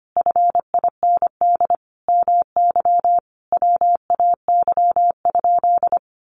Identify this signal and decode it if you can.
Morse code, heard as a keyed tone, about 25 wpm, 700 Hz. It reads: FIND MY WAY?
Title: Sentences from Top 200 Words 25wpm